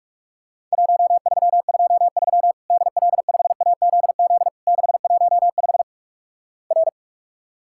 Morse code, heard as a keyed tone, about 47 wpm, 700 Hz. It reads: 1323 DL5AZZ 615 R
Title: Worked All Europe DX Contest - QTC Exchange - 45wpm